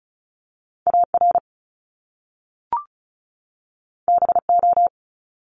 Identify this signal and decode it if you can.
Morse code, heard as a keyed tone, about 36 wpm, 700 Hz. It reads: AR BO
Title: Top Two Letter Combinations in Words 35wpm